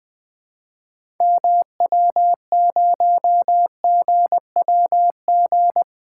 Morse code, heard as a keyed tone, about 20 wpm, 700 Hz. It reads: MW0GWG
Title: 2023 POTA Activator Call Signs - 20wpm